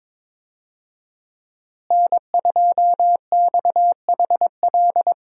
Morse code, heard as a keyed tone, about 22 wpm, 700 Hz. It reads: N2XHL
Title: USA Call Signs - 5 Characters - Repeated 2x - 22wpm